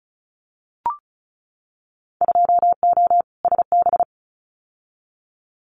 Morse code, heard as a keyed tone, about 34 wpm, 700 Hz. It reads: 2O SB
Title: Winter Field Day Exchanges 35wpm